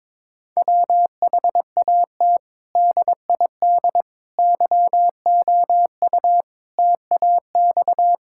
Morse code, heard as a keyed tone, about 22 wpm, 700 Hz. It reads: WHAT DID YOU TAX
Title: Sentences from Top 1000 Words - Review Part 5 - 22wpm